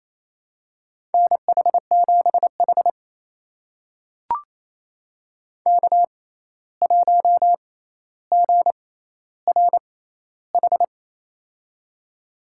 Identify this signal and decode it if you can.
Morse code, heard as a keyed tone, about 27 wpm, spaced as for 13 wpm, 700 Hz. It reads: NH7H K1GRH
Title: USA Call Signs - 6x Character Spacing - 28wpm